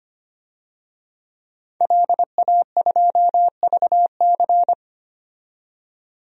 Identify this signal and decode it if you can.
Morse code, heard as a keyed tone, about 25 wpm, 700 Hz. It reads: LA2VC
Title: DXCC-100 Call Signs - 5 Characters - 25wpm